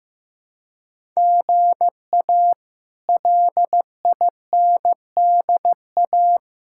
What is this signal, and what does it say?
Morse code, heard as a keyed tone, about 15 wpm, 700 Hz. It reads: GA LINDA
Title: K1USN SST QSO Practice - 15wpm